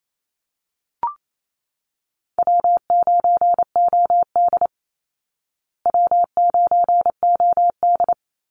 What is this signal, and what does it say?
Morse code, heard as a keyed tone, about 28 wpm, 700 Hz. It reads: W9OD W9OD
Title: Call Signs Repeated 2x - 28wpm